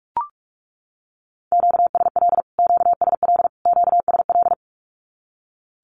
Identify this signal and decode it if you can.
Morse code, heard as a keyed tone, about 44 wpm, 700 Hz. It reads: QSL QSL QSL
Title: Q-Codes Repeated 3x 45wpm